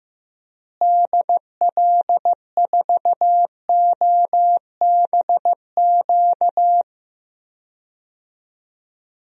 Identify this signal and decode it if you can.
Morse code, heard as a keyed tone, about 15 wpm, 700 Hz. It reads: DL4OBQ
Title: USA Call Signs - 6 Characters - Repeated 3x - 15wpm